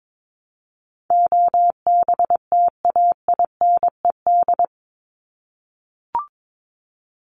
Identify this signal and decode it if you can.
Morse code, heard as a keyed tone, about 22 wpm, 700 Hz. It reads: OBTAINED
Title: ED Suffix - Top 100 Words with Base Words 22wpm